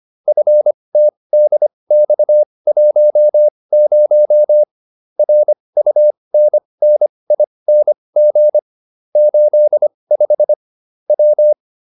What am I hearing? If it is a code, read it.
Morse code, heard as a keyed tone, about 25 wpm, spaced as for 20 wpm, 600 Hz. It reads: FTDX10 RUNNING 85 W